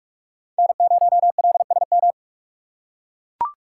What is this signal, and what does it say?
Morse code, heard as a keyed tone, about 44 wpm, 700 Hz. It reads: N0LSM
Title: USA Call Signs - 5 Characters - 4x Character Spacing - 45wpm